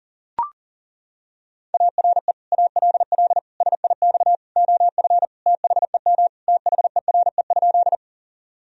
Morse code, heard as a keyed tone, about 40 wpm, 700 Hz. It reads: ARE ALL SIX OF THEM THERE?
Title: Sentences from Top 1000 Words - Review Part 2 - 40wpm